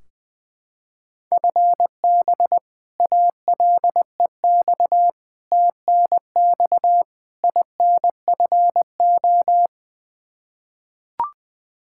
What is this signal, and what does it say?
Morse code, heard as a keyed tone, about 20 wpm, 700 Hz. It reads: FB ALEX TNX INFO
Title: Practice QSO - Exchange 3 - More Info - 20wpm